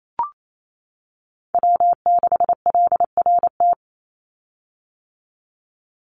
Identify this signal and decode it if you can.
Morse code, heard as a keyed tone, about 28 wpm, 700 Hz. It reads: W6LRT
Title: USA Call Signs 28wpm